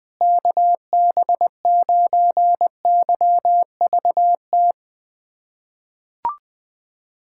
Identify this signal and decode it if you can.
Morse code, heard as a keyed tone, about 20 wpm, 700 Hz. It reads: KB9YVT
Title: USA Callsign DE Callsign - 6 Characters - 20wpm